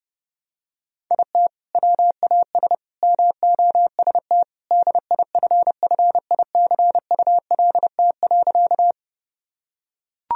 Morse code, heard as a keyed tone, about 30 wpm, 700 Hz. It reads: IT WAS MOST DIFFICULT.
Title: Sentences from Top 1000 Words - Review Part 1 - 30wpm